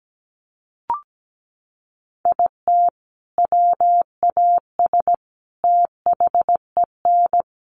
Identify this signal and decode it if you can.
Morse code, heard as a keyed tone, about 17 wpm, 700 Hz. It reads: IT WAS THEN
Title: Sentences from Top 1500 Words 17wpm